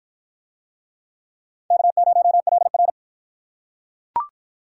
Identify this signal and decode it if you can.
Morse code, heard as a keyed tone, about 43 wpm, 700 Hz. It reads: K0LR
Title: Speed Racing - USA Call Signs 35wpm